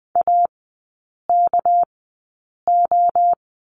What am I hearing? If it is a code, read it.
Morse code, heard as a keyed tone, about 20 wpm, spaced as for 8 wpm, 700 Hz. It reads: AKO
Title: 198 Lesson 32e - Letter k - Three Characters 20wpm